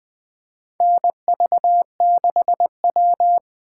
Morse code, heard as a keyed tone, about 20 wpm, 700 Hz. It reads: NV6W
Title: USA Call Signs - 4 Characters - 3x Character Spacing - 20wpm